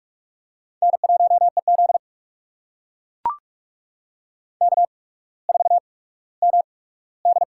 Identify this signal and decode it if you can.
Morse code, heard as a keyed tone, about 44 wpm, spaced as for 19 wpm, 690 Hz. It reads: N1EZ K4MD
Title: USA Call Signs - 4 Characters - 8x Character Spacing - 45wpm